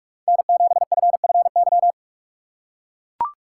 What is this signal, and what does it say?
Morse code, heard as a keyed tone, about 44 wpm, 700 Hz. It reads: N7FFY
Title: USA Call Signs - 5 Characters - 6x Character Spacing - 45wpm